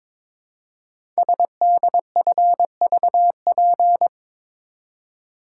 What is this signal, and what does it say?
Morse code, heard as a keyed tone, about 22 wpm, 700 Hz. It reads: SDFVP
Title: Code Groups - 5 Letter:Numbers - 22wpm